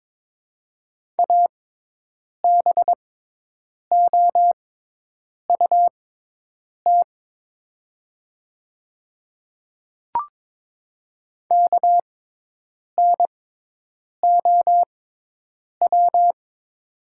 Morse code, heard as a keyed tone, about 22 wpm, spaced as for 8 wpm, 700 Hz. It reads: ABOUT KNOW
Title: 6x Character Spacing - Top 100 Words - 22wpm